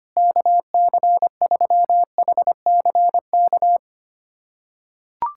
Morse code, heard as a keyed tone, about 25 wpm, 700 Hz. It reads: KC3HCK
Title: USA Callsign DE Callsign - 6 Characters - 25wpm